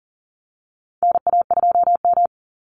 Morse code, heard as a keyed tone, about 40 wpm, 700 Hz. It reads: NA2M
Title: Call Signs Repeated 2x - 40wpm